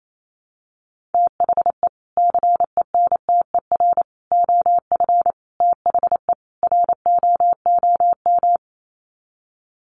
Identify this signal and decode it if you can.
Morse code, heard as a keyed tone, about 28 wpm, 700 Hz. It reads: THE CENTER OF THE ROOM
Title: Sets of 5 Words Encore 28wpm